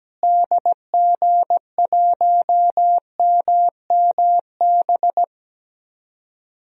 Morse code, heard as a keyed tone, about 17 wpm, 700 Hz. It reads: DG1MMB
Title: DXCC-100 Call Signs - 6 Characters - Repeated 2x - 17wpm